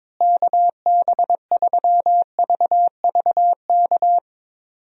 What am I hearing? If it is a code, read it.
Morse code, heard as a keyed tone, about 22 wpm, 700 Hz. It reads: KB3VVK
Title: USA Call Signs - 6 Characters - Repeated 3x - 22wpm